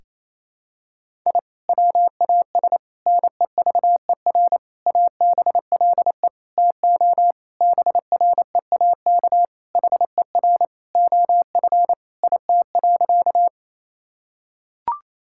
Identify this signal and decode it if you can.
Morse code, heard as a keyed tone, about 28 wpm, 700 Hz. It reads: I WAS NEVER ABLE TO BREAK HER OF IT.
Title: Sentences from Top 1000 Words - Review Part 6 - 28wpm